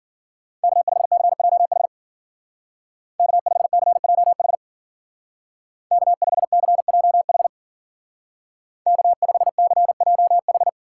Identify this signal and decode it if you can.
Morse code, heard as a keyed tone, about 50 wpm, 700 Hz. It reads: K5CJH K5CJH K5CJH K5CJH
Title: Speed Racing - USA 1x3 Call Signs 40wpm